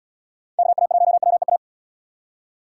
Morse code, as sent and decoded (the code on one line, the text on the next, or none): -.- .- .---- .-- . .-
KA1WEA